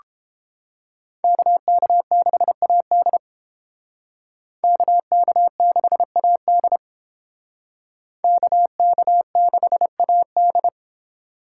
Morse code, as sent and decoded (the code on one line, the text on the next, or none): -.- -.- -.... .- -.. / -.- -.- -.... .- -.. / -.- -.- -.... .- -..
KK6AD KK6AD KK6AD